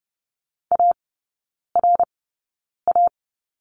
Morse code, heard as a keyed tone, about 30 wpm, spaced as for 9 wpm, 690 Hz. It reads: ARA
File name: Lesson 16e - Letter c - Three Characters 30wpm